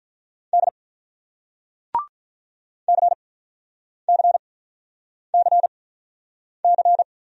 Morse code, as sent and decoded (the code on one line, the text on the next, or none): -.. / -.-. -.-. -.-. -.-.
D CCCC